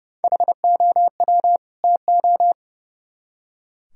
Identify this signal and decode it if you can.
Morse code, heard as a keyed tone, about 37 wpm, 700 Hz. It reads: HOW TO
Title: Speed-Racing - Sets of 2 Words 30wpm